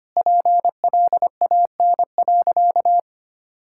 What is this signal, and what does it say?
Morse code, heard as a keyed tone, about 25 wpm, 700 Hz. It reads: PLAN.
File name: practice / 25wpm / Sentences with 4-Word Phrases - 25wpm